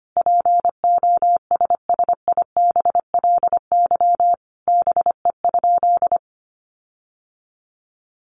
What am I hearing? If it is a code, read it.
Morse code, heard as a keyed tone, about 25 wpm, 700 Hz. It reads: POSSIBLY BE?
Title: Sentences from Top 1300 Words 25wpm